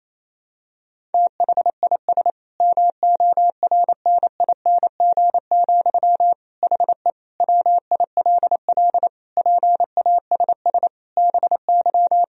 Morse code, heard as a keyed tone, about 28 wpm, 700 Hz. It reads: THIS MORNING, HE WILL PASS BY
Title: Sentences from Top 1000 Words - Review Part 1 - 28wpm